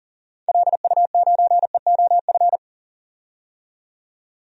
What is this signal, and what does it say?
Morse code, heard as a keyed tone, about 39 wpm, 700 Hz. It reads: LU9EOF
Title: DXCC-100 Call Signs - 6 Characters - 40wpm